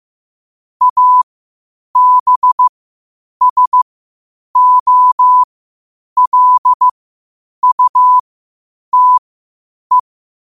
Morse code, heard as a keyed tone, about 15 wpm, spaced as for 8 wpm, 1000 Hz. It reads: ABSOLUTE
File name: qotdfiles / 8wpm